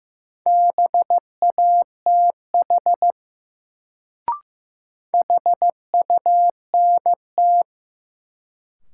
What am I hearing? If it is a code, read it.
Morse code, heard as a keyed tone, about 15 wpm, 700 Hz. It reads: BATH HUNT
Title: Sentences from Top 1600 Words - New Word Drill - 15wpm